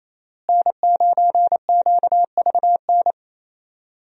Morse code, heard as a keyed tone, about 27 wpm, 700 Hz. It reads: N9QVN
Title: USA Call Signs - 5 Characters - Repeated 2x - 28wpm